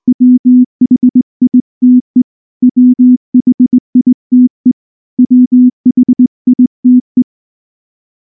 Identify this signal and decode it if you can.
Morse code, heard as a keyed tone, about 19 wpm, 250 Hz. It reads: WHITE WHITE WHITE